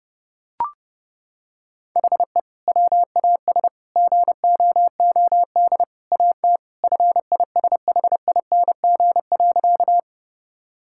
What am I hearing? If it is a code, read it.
Morse code, heard as a keyed tone, about 30 wpm, 700 Hz. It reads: HE WAS GOOD AT FISHING.
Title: Sentences from Top 1000 Words - Review Part 5 - 30wpm